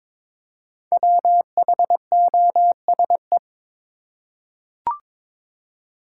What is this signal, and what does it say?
Morse code, heard as a keyed tone, about 22 wpm, 700 Hz. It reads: WHOSE
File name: 5 Letter Words - First 3 Letters Given - 22wpm